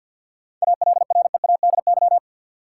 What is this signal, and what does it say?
Morse code, heard as a keyed tone, about 50 wpm, 700 Hz. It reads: ALREADY
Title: Seven Letter Words - 50wpm